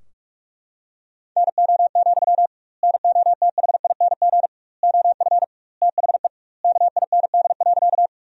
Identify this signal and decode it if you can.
Morse code, heard as a keyed tone, about 42 wpm, 700 Hz. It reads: NO, NOTHING OF THE KIND.
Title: 2x Word-Spacing - Sentences from Top 500 Words - 45wpm